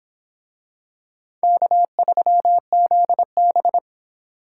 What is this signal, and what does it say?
Morse code, heard as a keyed tone, about 25 wpm, 700 Hz. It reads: K3ZB